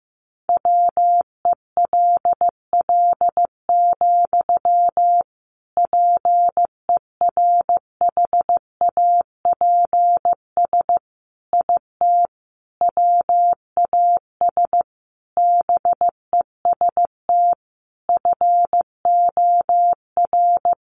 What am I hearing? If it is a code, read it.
Morse code, heard as a keyed tone, about 15 wpm, 700 Hz. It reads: WELL, PERHAPS IT WAS BEST FOR
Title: Sentences from Top 500 Words 15wpm